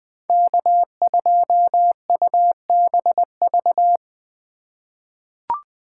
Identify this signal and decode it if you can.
Morse code, heard as a keyed tone, about 20 wpm, 700 Hz. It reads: K2UBV
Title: USA Call Signs - 5 Characters - 3x Character Spacing - 20wpm